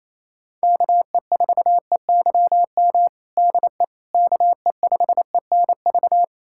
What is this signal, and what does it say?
Morse code, heard as a keyed tone, about 28 wpm, 700 Hz. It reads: KE4EYM DE KE5ENV
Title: USA Callsign DE Callsign - 6 Characters - 28wpm